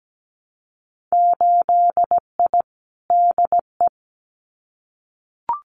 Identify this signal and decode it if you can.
Morse code, heard as a keyed tone, about 17 wpm, 700 Hz. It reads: 8I DE
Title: Winter Field Day Exchanges Encore 17wpm